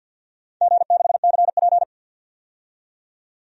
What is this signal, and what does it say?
Morse code, heard as a keyed tone, about 52 wpm, 700 Hz. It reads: G6CP
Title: DXCC-100 Call Signs - 4 Characters - 50wpm